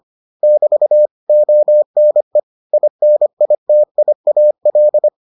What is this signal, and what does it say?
Morse code, heard as a keyed tone, about 25 wpm, 600 Hz. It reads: <BT> ONE INITIAL